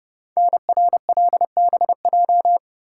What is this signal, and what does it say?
Morse code, heard as a keyed tone, about 30 wpm, 700 Hz. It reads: NRLBJ
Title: Code Groups - 5 Letters - 30wpm